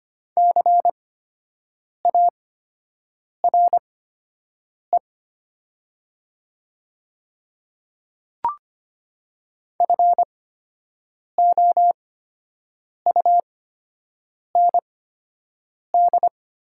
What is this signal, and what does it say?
Morse code, heard as a keyed tone, about 25 wpm, spaced as for 7 wpm, 700 Hz. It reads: CARE FOUND
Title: 8x Character Spacing - Top 300 Words - 25wpm